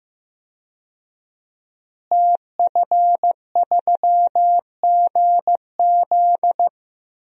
Morse code, heard as a keyed tone, about 15 wpm, 700 Hz. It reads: TF3GZ
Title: DXCC-100 Call Signs - 5 Characters - Repeated 3x - 15wpm